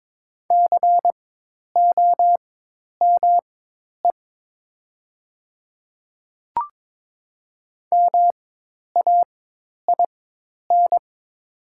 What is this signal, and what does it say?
Morse code, heard as a keyed tone, about 22 wpm, spaced as for 10 wpm, 700 Hz. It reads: COME MAIN
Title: 4x Character Spacing - Top 300 Words - 22wpm